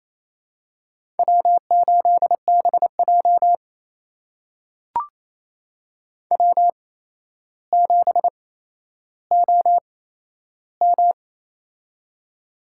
Morse code, heard as a keyed tone, about 28 wpm, spaced as for 12 wpm, 700 Hz. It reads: W8BJ W7OM
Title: USA Call Signs - 4 Characters - 8x Character Spacing - 28wpm